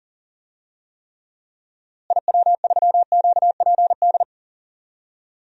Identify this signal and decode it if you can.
Morse code, heard as a keyed tone, about 40 wpm, 700 Hz. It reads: IW3QPD
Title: DXCC-100 Call Signs - 6 Characters - Repeated 3x - 40wpm